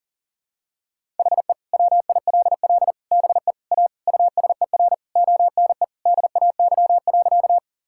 Morse code, heard as a keyed tone, about 40 wpm, 700 Hz. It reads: HE WILL BE A USER ONE DAY.
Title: Sentences from Top 1000 Words - Review Part 5 - 40wpm